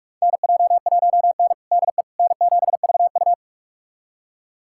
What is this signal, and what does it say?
Morse code, heard as a keyed tone, about 41 wpm, 700 Hz. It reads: NJ1N DE N7VU
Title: USA Callsign DE Callsign - 4 Characters - 45wpm